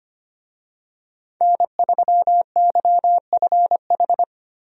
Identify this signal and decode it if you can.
Morse code, heard as a keyed tone, about 25 wpm, 700 Hz. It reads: N3YFH
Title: USA Call Signs - 5 Characters - 25wpm